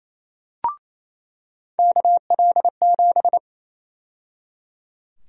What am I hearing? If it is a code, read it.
Morse code, heard as a keyed tone, about 28 wpm, 700 Hz. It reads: KL7
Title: USA Call Sign Prefixes - 28wpm